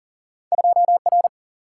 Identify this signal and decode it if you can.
Morse code, heard as a keyed tone, about 38 wpm, 700 Hz. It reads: 2R
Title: Single-Letter Single-Number 40wpm